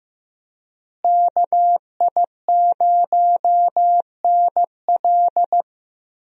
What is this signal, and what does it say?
Morse code, heard as a keyed tone, about 15 wpm, 700 Hz. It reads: KI0NL
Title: USA Call Signs - 5 Characters - Repeated 3x - 15wpm